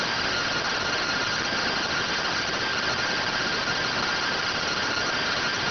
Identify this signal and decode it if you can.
Two-tone teleprinter signal, about 170 Hz shift.